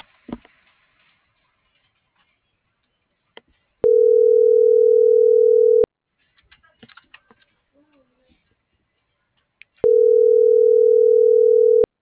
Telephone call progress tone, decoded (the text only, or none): ringback tone